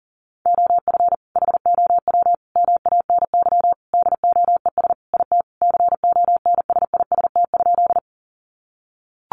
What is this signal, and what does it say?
Morse code, heard as a keyed tone, about 40 wpm, 700 Hz. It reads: OF HOW MANY DOES IT CONSIST?
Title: Sentences from Top 1400 Words 40wpm